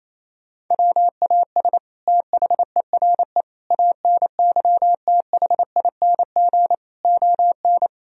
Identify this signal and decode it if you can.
Morse code, heard as a keyed tone, about 28 wpm, 700 Hz. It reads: WAS THERE ANYTHING ON
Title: Sentences from Top 1000 Words - Review Part 3 - 28wpm